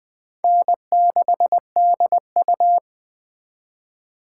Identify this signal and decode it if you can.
Morse code, heard as a keyed tone, about 21 wpm, 700 Hz. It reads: N6DU
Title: Speed Racing - USA 1x2 Call Signs 17wpm